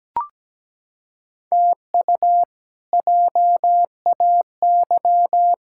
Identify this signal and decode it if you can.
Morse code, heard as a keyed tone, about 17 wpm, 700 Hz. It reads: TU JAY
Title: K1USN SST QSO Practice - 17wpm